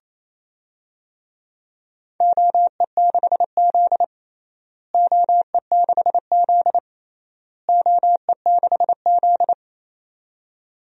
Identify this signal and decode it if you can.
Morse code, heard as a keyed tone, about 28 wpm, 700 Hz. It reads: OE6Z OE6Z OE6Z
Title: DXCC-100 Call Signs - 4 Characters - Repeated 3x - 28wpm